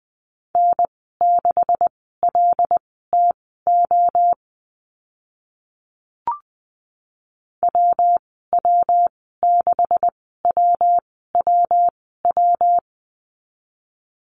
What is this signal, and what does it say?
Morse code, heard as a keyed tone, about 20 wpm, spaced as for 15 wpm, 700 Hz. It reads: N6LTO WW6WWW
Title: 140 Lesson 23g - Letter f - Call Signs 20wpm